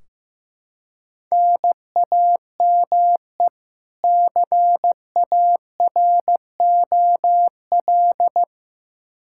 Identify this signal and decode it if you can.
Morse code, heard as a keyed tone, about 15 wpm, 700 Hz. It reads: NAME CAROL
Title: Practice QSO - Exchange 1 - Beginning - 15wpm